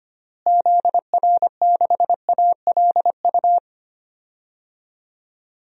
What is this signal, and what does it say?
Morse code, heard as a keyed tone, about 25 wpm, 700 Hz. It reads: ZR6ALU
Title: DXCC-100 Call Signs - 6 Characters - Repeated 3x - 25wpm